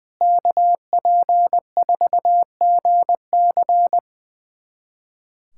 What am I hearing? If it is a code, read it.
Morse code, heard as a keyed tone, about 20 wpm, 700 Hz. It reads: KP4GC
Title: USA Callsign DE Callsign - 5 Characters - 20wpm